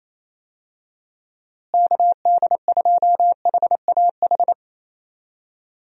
Morse code, heard as a keyed tone, about 27 wpm, 700 Hz. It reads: KD2HAH
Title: USA Call Signs - 6 Characters - Repeated 3x - 28wpm